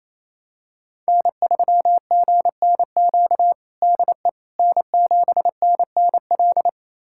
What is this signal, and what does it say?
Morse code, heard as a keyed tone, about 28 wpm, 700 Hz. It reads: N3GNQ DE N7NNL
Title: USA Callsign DE Callsign - 5 Characters - 28wpm